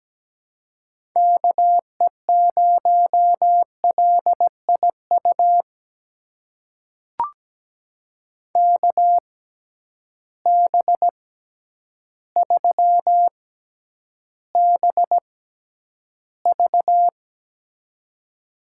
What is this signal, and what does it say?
Morse code, heard as a keyed tone, about 17 wpm, spaced as for 9 wpm, 700 Hz. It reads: KE0LIU KB3BV
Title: USA Call Signs - 6x Character Spacing - 17wpm